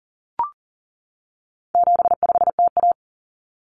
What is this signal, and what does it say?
Morse code, heard as a keyed tone, about 40 wpm, 700 Hz. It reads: 75TA
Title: Common Cut Numbers - 4-Digit 40wpm